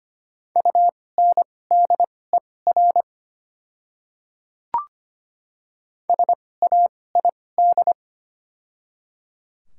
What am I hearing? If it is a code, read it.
Morse code, heard as a keyed tone, about 25 wpm, spaced as for 18 wpm, 700 Hz. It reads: UNDER SAID
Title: 2x Character Spacing - Top 300 Words - 25wpm